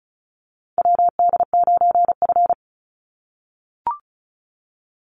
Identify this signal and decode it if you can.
Morse code, heard as a keyed tone, about 34 wpm, 700 Hz. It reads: WD9F
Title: NA SOTA Chaser Callsigns 35wpm